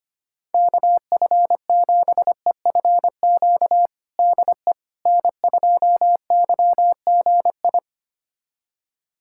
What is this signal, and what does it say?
Morse code, heard as a keyed tone, about 25 wpm, 700 Hz. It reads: KF7EFQ DE N2YGI
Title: USA Callsign DE Callsign 25wpm